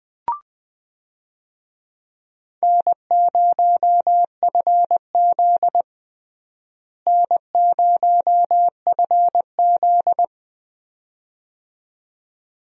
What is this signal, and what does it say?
Morse code, heard as a keyed tone, about 20 wpm, 700 Hz. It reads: N0FZ N0FZ
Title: USA Call Signs - 4 Characters - Repeated 2x - 20wpm